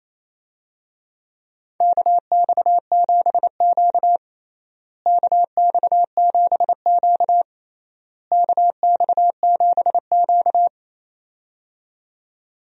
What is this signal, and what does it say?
Morse code, heard as a keyed tone, about 28 wpm, 700 Hz. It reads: KX7Q KX7Q KX7Q
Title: USA Call Signs - 4 Characters - Repeated 3x - 28wpm